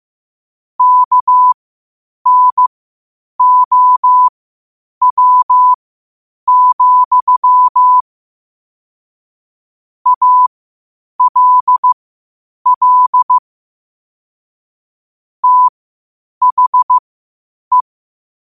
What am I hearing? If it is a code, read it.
Morse code, heard as a keyed tone, about 15 wpm, spaced as for 8 wpm, 1000 Hz. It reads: KNOW, ALL THE
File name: qotdfiles / 8wpm